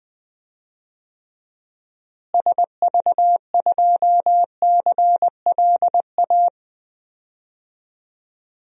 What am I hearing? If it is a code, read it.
Morse code, heard as a keyed tone, about 20 wpm, 700 Hz. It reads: SV2CLA
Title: DXCC-100 Call Signs - 6 Characters - Repeated 2x - 20wpm